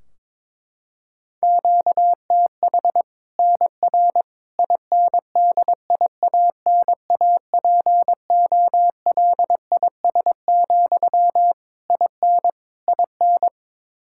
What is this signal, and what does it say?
Morse code, heard as a keyed tone, about 22 wpm, 700 Hz. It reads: QTH NR INDIANAPOLIS, IN IN
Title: Practice QSO - Exchange 1 - Beginning - 22wpm